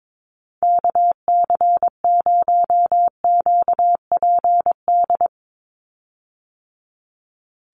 Morse code, heard as a keyed tone, about 22 wpm, 700 Hz. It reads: KC0QPD
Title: USA Call Signs 22wpm